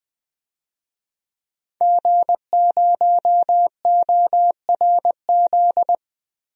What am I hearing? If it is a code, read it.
Morse code, heard as a keyed tone, about 20 wpm, 700 Hz. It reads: G0ORZ
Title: DXCC-100 Call Signs - 5 Characters - 20wpm